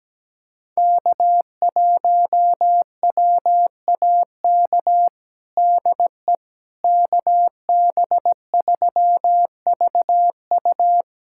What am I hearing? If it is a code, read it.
Morse code, heard as a keyed tone, about 17 wpm, 700 Hz. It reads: K1WAK DE KB3VU
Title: USA Callsign DE Callsign - 5 Characters - 17wpm